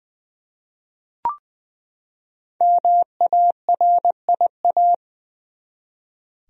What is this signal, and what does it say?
Morse code, heard as a keyed tone, about 20 wpm, 700 Hz. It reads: MARIA
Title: USA Common Names - 20wpm